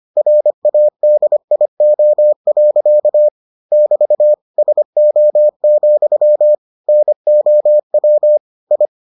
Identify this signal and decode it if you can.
Morse code, heard as a keyed tone, about 25 wpm, 600 Hz. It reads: RADIO. <BT> SO, NOW I